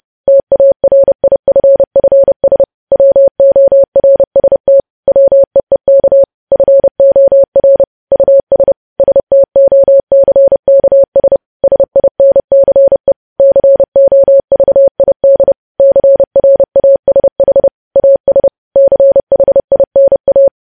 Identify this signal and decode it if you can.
Morse code, heard as a keyed tone, about 30 wpm, 550 Hz. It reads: TARIFFS WORST WEEK FOR US STOCKS SINCE COVID CRASH AS CHINA
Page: news headlines in morse code at 30 wpm